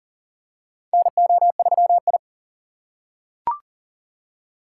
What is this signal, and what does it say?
Morse code, heard as a keyed tone, about 40 wpm, 700 Hz. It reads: NO3I
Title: USA Call Signs - 3x Character Spacing - 40wpm